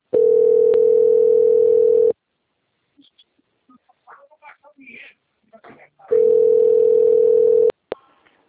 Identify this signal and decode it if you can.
Ringback tone (North America), 440 + 480 Hz, with speech.
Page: Cosmopolis - SP